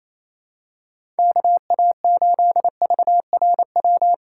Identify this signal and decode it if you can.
Morse code, heard as a keyed tone, about 27 wpm, 700 Hz. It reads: KA8VRW
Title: USA Call Signs - 6 Characters - 6x Character Spacing - 28wpm